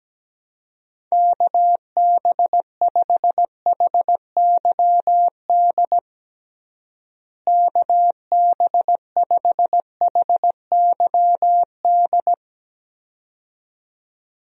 Morse code, heard as a keyed tone, about 17 wpm, 700 Hz. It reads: KB5HYD KB5HYD
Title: USA Call Signs - 6 Characters - Repeated 2x - 17wpm